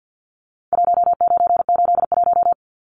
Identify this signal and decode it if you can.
Morse code, heard as a keyed tone, about 50 wpm, 700 Hz. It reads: 1981
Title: Numbers - Contest Style - 4-Digit 50wpm